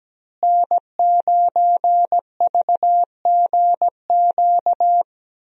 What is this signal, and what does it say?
Morse code, heard as a keyed tone, about 17 wpm, 700 Hz. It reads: N9VGQ
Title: USA Call Signs - 5 Characters - Repeated 2x - 17wpm